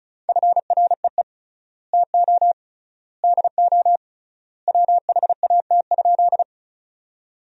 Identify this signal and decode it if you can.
Morse code, heard as a keyed tone, about 34 wpm, 700 Hz. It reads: FREE TO DO WHAT?
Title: 3x Word-Spacing - Sentences from Top 500 Words - 35wpm